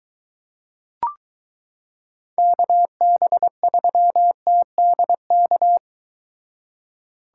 Morse code, heard as a keyed tone, about 23 wpm, 700 Hz. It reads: KB3TDK